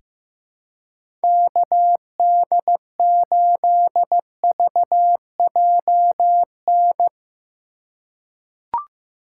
Morse code, heard as a keyed tone, about 15 wpm, 700 Hz. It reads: KD8VJN
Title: USA Call Signs - 6 Characters - 4x Character Spacing - 15wpm